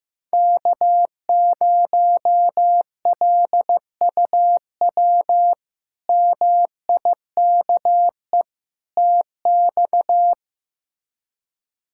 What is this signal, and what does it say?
Morse code, heard as a keyed tone, about 16 wpm, 700 Hz. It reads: K0LUW MIKE TX
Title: K1USN SST QSO Practice - 15wpm